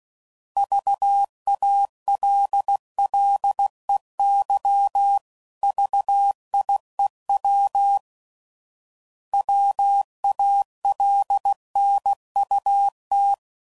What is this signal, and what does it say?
Morse code, heard as a keyed tone, about 16 wpm, 800 Hz. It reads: VALLEY VIEW WALNUT
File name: CW Code Groups @15WPM / fromSM's / Cities